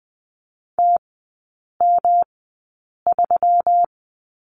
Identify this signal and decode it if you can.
Morse code, heard as a keyed tone, about 20 wpm, spaced as for 8 wpm, 690 Hz. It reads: TM3
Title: 145 Lesson 24e - Letter y - Three Characters 20wpm